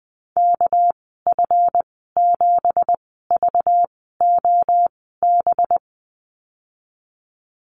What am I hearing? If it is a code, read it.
Morse code, heard as a keyed tone, about 20 wpm, spaced as for 14 wpm, 700 Hz. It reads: KF7VOB
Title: 245 Lesson 39g - Letter z - Call Signs 20wpm